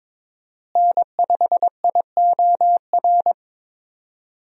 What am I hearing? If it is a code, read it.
Morse code, heard as a keyed tone, about 22 wpm, 700 Hz. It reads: N5IOR
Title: USA Call Signs - 5 Characters - Repeated 3x - 22wpm